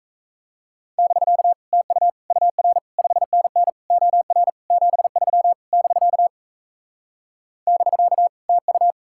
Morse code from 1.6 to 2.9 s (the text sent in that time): TU UR